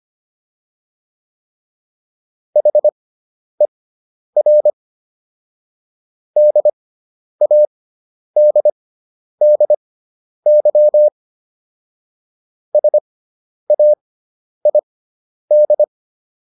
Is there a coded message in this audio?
Yes: Morse code from 2.6 to 15.8 s, heard as a keyed tone, about 25 wpm, spaced as for 10 wpm, 600 Hz. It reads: HER DADDY SAID